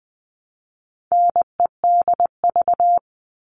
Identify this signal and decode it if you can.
Morse code, heard as a keyed tone, about 20 wpm, 700 Hz. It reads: NEDV
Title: All Cut Numbers - 4-Digit 20wpm